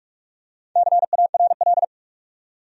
Morse code, heard as a keyed tone, about 46 wpm, 700 Hz. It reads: CARL
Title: USA Common Names - 45wpm